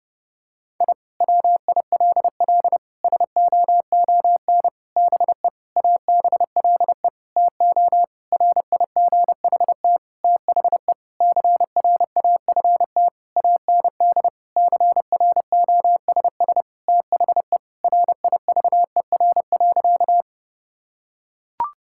Morse code, heard as a keyed tone, about 32 wpm, 700 Hz. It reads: I WILL SOON BE ABLE TO RIGHT THE CRAFT AND CROSS THE RIVER.
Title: Sentences from Top 1000 Words - Review Part 6 - 30wpm